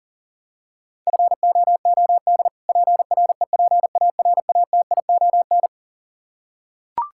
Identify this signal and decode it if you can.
Morse code, heard as a keyed tone, about 40 wpm, 700 Hz. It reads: FOOD PREPARATION
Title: K1USN SST QSO Practice 40wpm